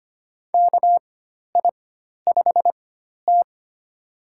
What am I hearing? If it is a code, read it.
Morse code, heard as a keyed tone, about 25 wpm, spaced as for 12 wpm, 700 Hz. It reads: KI5T
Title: USA Call Signs - 4 Characters - 4x Character Spacing - 25wpm